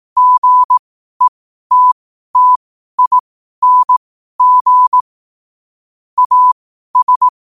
Morse code, heard as a keyed tone, about 18 wpm, spaced as for 12 wpm, 1000 Hz. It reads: GETTING AS
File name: qotdfiles / 12wpm